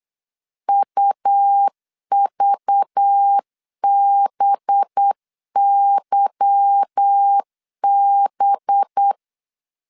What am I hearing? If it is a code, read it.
Morse code, heard as a keyed tone, about 8 wpm, 800 Hz. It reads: UVBYB
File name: CW / DET / 1 9 1500C S B 8wpm 5624